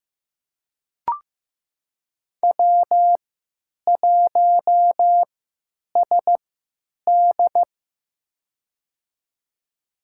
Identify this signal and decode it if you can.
Morse code, heard as a keyed tone, about 15 wpm, spaced as for 9 wpm, 700 Hz. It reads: W1SD
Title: USA Call Signs - 4 Characters - 3x Character Spacing - 15wpm